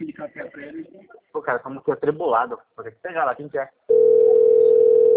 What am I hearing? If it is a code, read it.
Ringback tone (North America), 440 + 480 Hz, with speech.
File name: out- - - - - .16382